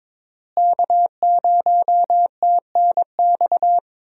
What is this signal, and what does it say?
Morse code, heard as a keyed tone, about 22 wpm, 700 Hz. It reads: K0TNX
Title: USA Call Signs - 5 Characters - Repeated 3x - 22wpm